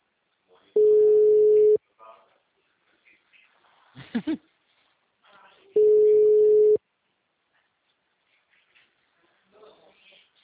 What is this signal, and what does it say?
Ringback tone (Europe, 425 Hz), with speech.
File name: out- -2017- - - .33